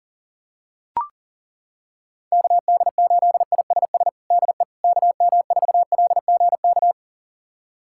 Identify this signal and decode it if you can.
Morse code, heard as a keyed tone, about 37 wpm, 700 Hz. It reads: KD8ISS DE KM4LGK
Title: USA Callsign DE Callsign - 6 Characters - 40wpm